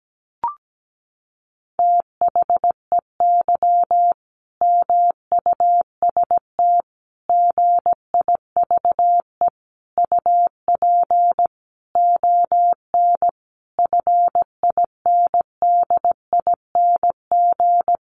Morse code, heard as a keyed tone, about 17 wpm, 700 Hz. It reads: THEY MUST GIVE UP ON FINDING
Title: Sentences from Top 1100 Words 17wpm